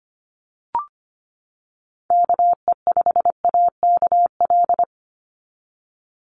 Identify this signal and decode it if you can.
Morse code, heard as a keyed tone, about 25 wpm, 700 Hz. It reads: KE5AKL
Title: NA SOTA Chaser Callsigns 25wpm